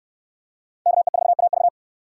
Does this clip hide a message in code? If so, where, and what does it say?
Yes: Morse code from 0.9 to 1.7 s, heard as a keyed tone, about 61 wpm, 700 Hz. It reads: KE4AV